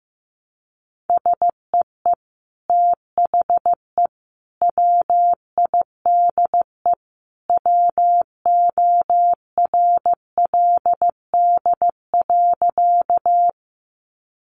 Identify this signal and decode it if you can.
Morse code, heard as a keyed tone, about 15 wpm, 700 Hz. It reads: SEE THE WIDE WORLD.
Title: Sentences from Top 900 Words 15wpm